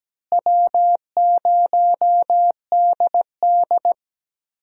Morse code, heard as a keyed tone, about 17 wpm, 700 Hz. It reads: W0DD
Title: USA Call Signs - 4 Characters - Repeated 2x - 17wpm